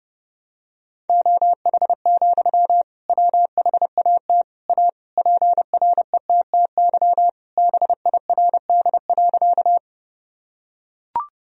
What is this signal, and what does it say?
Morse code, heard as a keyed tone, about 28 wpm, 700 Hz. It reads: OH, WHAT A PRETTY BIRD.
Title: Sentences from Top 1000 Words - Review Part 2 - 30wpm